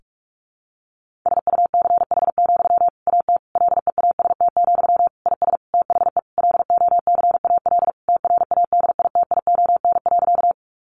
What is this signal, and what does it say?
Morse code, heard as a keyed tone, about 44 wpm, 700 Hz. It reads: SUCH, AT LEAST, IS THE LOCAL TRADITION.
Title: Sentences from Top 1500 Words 45wpm